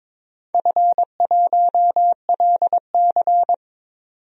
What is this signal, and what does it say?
Morse code, heard as a keyed tone, about 22 wpm, 700 Hz. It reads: F1LC
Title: DXCC-100 Call Signs - 4 Characters - 22wpm